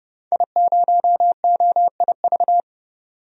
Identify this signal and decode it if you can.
Morse code, heard as a keyed tone, about 30 wpm, 700 Hz. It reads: I0OIV
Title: DXCC-100 Call Signs - 5 Characters - 30wpm